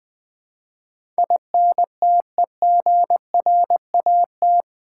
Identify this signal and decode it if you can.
Morse code, heard as a keyed tone, about 20 wpm, 700 Hz. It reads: INTEGRAT
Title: Build A Word - 4 Letters At A Time - 11-Letter Words - 20wpm